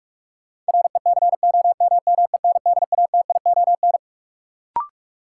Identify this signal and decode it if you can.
Morse code, heard as a keyed tone, about 43 wpm, 700 Hz. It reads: RECOMMENDATION
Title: Sentences from Top 1400 Words - New Word Drill Alt - 45wpm